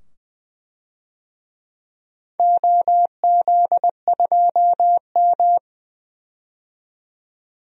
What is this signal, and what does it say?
Morse code, heard as a keyed tone, about 21 wpm, 700 Hz. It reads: OZ2M
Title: DXCC-100 Call Signs - 4 Characters - 20wpm